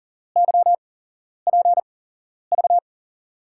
Morse code, heard as a keyed tone, about 39 wpm, spaced as for 11 wpm, 700 Hz. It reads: YPV
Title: USA Call Signs - 6 Characters - 8x Character Spacing - 40wpm